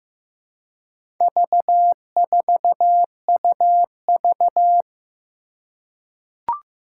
Morse code, heard as a keyed tone, about 15 wpm, 700 Hz. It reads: V4UV
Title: Four Challenging Letter-Number Combinations 15wpm